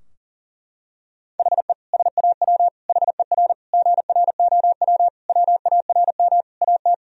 Morse code, heard as a keyed tone, about 40 wpm, 700 Hz. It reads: HE SAW HER GROW WARM AT
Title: Sentences from Top 1000 Words - Review Part 2 - 40wpm